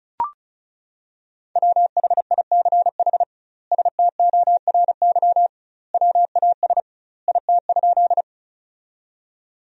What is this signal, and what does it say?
Morse code, heard as a keyed tone, about 36 wpm, 700 Hz. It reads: WHICH STORY WAS IT?
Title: 2x Word-Spacing - Sentences from Top 300 Words - 35wpm